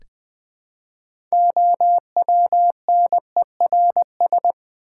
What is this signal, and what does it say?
Morse code, heard as a keyed tone, about 20 wpm, 700 Hz. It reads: OWNERS
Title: Build A Word - 3 Letters At A Time - 9-Letter Words - 20wpm